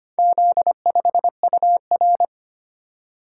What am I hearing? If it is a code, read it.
Morse code, heard as a keyed tone, about 25 wpm, 700 Hz. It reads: Z5UR
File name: practice / 25wpm / Four Letter-Number Combinations - 25wpm